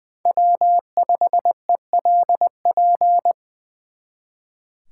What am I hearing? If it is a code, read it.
Morse code, heard as a keyed tone, about 24 wpm, 700 Hz. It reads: W5ELP
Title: Speed Racing - USA 1x3 Call Signs 20wpm